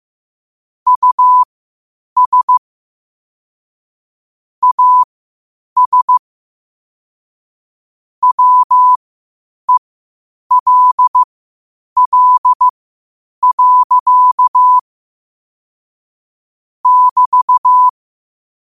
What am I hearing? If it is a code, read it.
Morse code, heard as a keyed tone, about 15 wpm, spaced as for 8 wpm, 1000 Hz. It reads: US AS WELL. <BT>